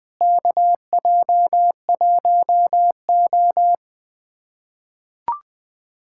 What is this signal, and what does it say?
Morse code, heard as a keyed tone, about 20 wpm, 700 Hz. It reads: KJ1O
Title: USA Callsign DE Callsign - 4 Characters - 20wpm